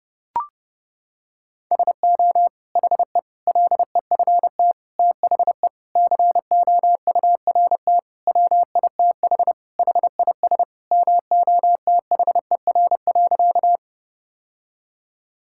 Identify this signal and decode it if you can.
Morse code, heard as a keyed tone, about 32 wpm, 700 Hz. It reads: SO HE LEFT THE COURT WITH HIS MOTHER.
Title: Sentences from Top 1600 Words 30wpm